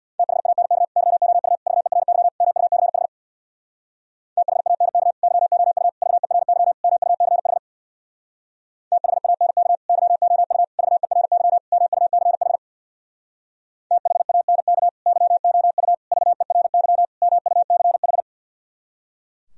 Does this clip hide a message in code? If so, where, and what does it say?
Yes: Morse code from 0.2 to 18.2 s, heard as a keyed tone, about 64 wpm, 700 Hz. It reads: THANK YOU VERY MUCH THANK YOU VERY MUCH THANK YOU VERY MUCH THANK YOU VERY MUCH